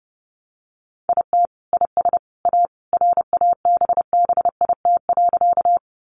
Morse code, heard as a keyed tone, about 30 wpm, 700 Hz. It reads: IT IS A RABBIT.
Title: Easy Phrases 30wpm